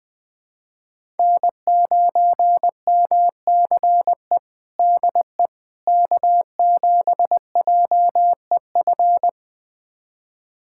Morse code, heard as a keyed tone, about 20 wpm, 700 Hz. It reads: N9MCE DE K7JEF
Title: USA Callsign DE Callsign - 5 Characters - 20wpm